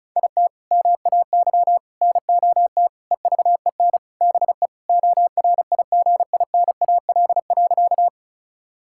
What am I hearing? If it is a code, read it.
Morse code, heard as a keyed tone, about 36 wpm, 700 Hz. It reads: IT MAY NOT EVEN BE ORIGINAL.
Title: Sentences from Top 1000 Words - Review Part 6 - 35wpm